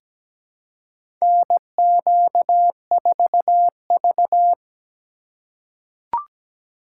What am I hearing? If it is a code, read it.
Morse code, heard as a keyed tone, about 17 wpm, 700 Hz. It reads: NQ4V
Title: USA Call Signs - 4 Characters - 2x Character Spacing - 17wpm